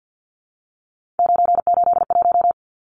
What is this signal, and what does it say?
Morse code, heard as a keyed tone, about 48 wpm, 700 Hz. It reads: 981
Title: Numbers - 3-Digit 50wpm